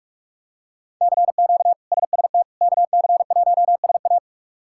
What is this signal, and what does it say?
Morse code, heard as a keyed tone, about 44 wpm, 700 Hz. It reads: CQ SST KC1SA
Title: K1USN SST QSO Practice - 45wpm